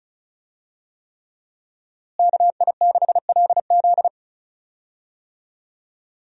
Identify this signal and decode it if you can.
Morse code, heard as a keyed tone, about 36 wpm, 700 Hz. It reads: KI6LZ
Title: USA Call Signs - 5 Characters - Repeated 3x - 35wpm